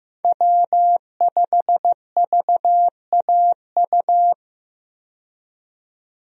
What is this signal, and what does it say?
Morse code, heard as a keyed tone, about 15 wpm, 700 Hz. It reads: W5VAU
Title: USA Call Signs - 5 Characters - 15wpm